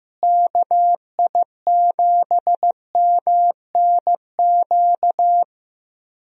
Morse code, heard as a keyed tone, about 15 wpm, 700 Hz. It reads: KI7MNQ
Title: USA Call Signs - 6 Characters - Repeated 2x - 15wpm